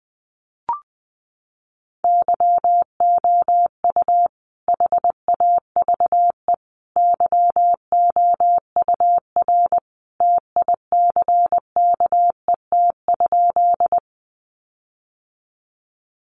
Morse code, heard as a keyed tone, about 20 wpm, 700 Hz. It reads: YOU HAVE YOUR TICKET?
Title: Sentences from Top 1400 Words 20wpm